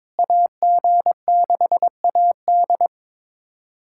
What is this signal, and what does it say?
Morse code, heard as a keyed tone, about 22 wpm, 700 Hz. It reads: AG6AD
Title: USA Callsign DE Callsign - 5 Characters - 22wpm